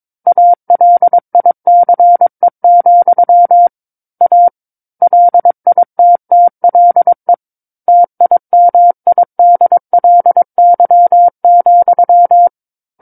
Morse code, heard as a keyed tone, about 22 wpm, 700 Hz. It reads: ALICE, A LITTLE TIMIDLY,
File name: 06-Chapter-06 22 WPM